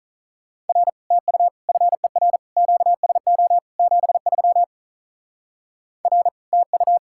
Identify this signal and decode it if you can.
Morse code, heard as a keyed tone, about 44 wpm, 700 Hz. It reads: R TU FER QSO 73 R TU